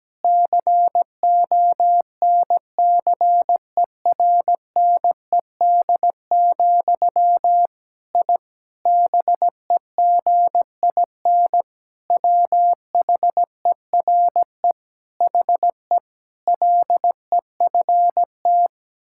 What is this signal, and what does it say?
Morse code, heard as a keyed tone, about 17 wpm, 700 Hz. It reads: CONCERNED, I BEGIN WHERE HE LEFT
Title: Sentences from Top 1000 Words - Review Part 6 - 17wpm